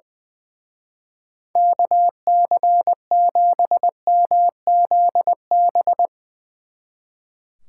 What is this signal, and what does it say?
Morse code, heard as a keyed tone, about 22 wpm, 700 Hz. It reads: KC7MZB
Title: Speed Racing - USA 2x3 Call Signs 20wpm